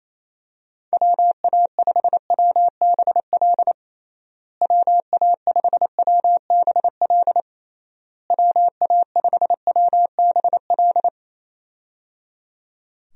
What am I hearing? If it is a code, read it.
Morse code, heard as a keyed tone, about 27 wpm, 700 Hz. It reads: WA5WBL WA5WBL WA5WBL
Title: USA Call Signs - 6 Characters - Repeated 3x - 28wpm